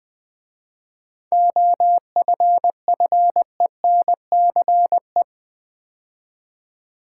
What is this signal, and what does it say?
Morse code, heard as a keyed tone, about 20 wpm, 700 Hz. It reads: OFFENCE
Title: 7 Letter Words - First 3 Letters Given - 20wpm